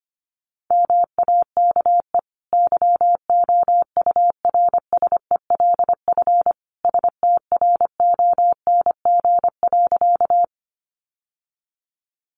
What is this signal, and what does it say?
Morse code, heard as a keyed tone, about 25 wpm, 700 Hz. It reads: MAKE YOURSELF STRONG.
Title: Sentences from Top 600 Words 25wpm